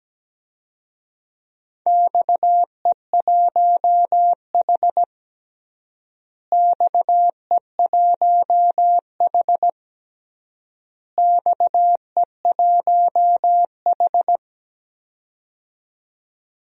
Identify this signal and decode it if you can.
Morse code, heard as a keyed tone, about 17 wpm, 700 Hz. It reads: XE1H XE1H XE1H
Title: DXCC-100 Call Signs - 4 Characters - Repeated 3x - 17wpm